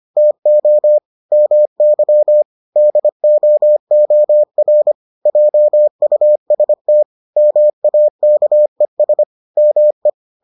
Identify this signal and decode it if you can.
Morse code, heard as a keyed tone, about 25 wpm, 600 Hz. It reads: TO MY DOOR JUST MAKES ME